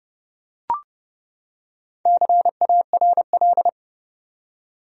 Morse code, heard as a keyed tone, about 30 wpm, 700 Hz. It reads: CARL
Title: USA Common Names - 30wpm